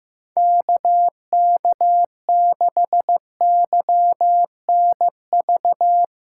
Morse code, heard as a keyed tone, about 15 wpm, 700 Hz. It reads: KK6YNV
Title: Speed Racing - USA 2x3 Call Signs 15wpm